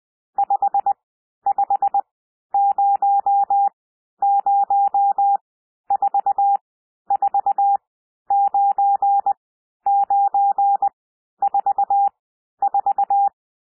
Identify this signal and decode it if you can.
Morse code, heard as a keyed tone, about 20 wpm, spaced as for 11 wpm, 800 Hz. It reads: 5500449944
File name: SOUND NUMBERS 20 wpm